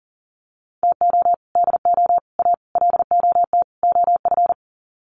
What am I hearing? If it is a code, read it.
Morse code, heard as a keyed tone, about 40 wpm, 700 Hz. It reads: TO DO A LOT OF
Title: Sets of 5 Words Encore 40wpm